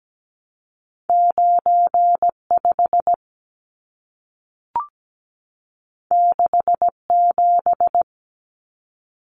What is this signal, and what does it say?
Morse code, heard as a keyed tone, about 17 wpm, 700 Hz. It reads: 95 67
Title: Numbers - 2-Digit 17wpm